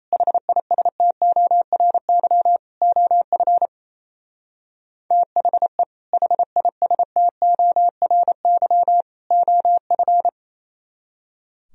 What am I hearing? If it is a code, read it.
Morse code, heard as a keyed tone, about 32 wpm, 700 Hz. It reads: HISTORY OF THE HISTORY OF
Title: Speed-Racing - Sets of 3 Words 28wpm